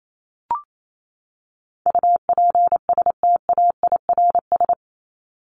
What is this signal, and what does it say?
Morse code, heard as a keyed tone, about 28 wpm, 700 Hz. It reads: UPSTAIRS
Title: Top 1751-2000 Words 28wpm